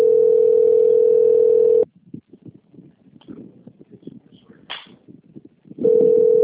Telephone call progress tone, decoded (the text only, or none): ringback tone